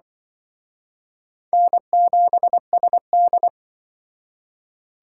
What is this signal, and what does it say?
Morse code, heard as a keyed tone, about 24 wpm, 700 Hz. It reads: N7SD
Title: Speed Racing - USA 1x2 Call Signs 20wpm